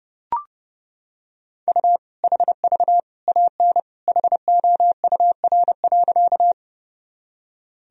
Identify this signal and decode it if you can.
Morse code, heard as a keyed tone, about 32 wpm, 700 Hz. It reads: U HV AN HOUR.
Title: Sentences from Top 400 Words and Abbreviations 30wpm